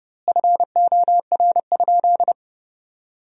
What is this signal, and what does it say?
Morse code, heard as a keyed tone, about 32 wpm, 700 Hz. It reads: FOR?
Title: 2x Word-Spacing - Sentences from Top 200 Words - 30wpm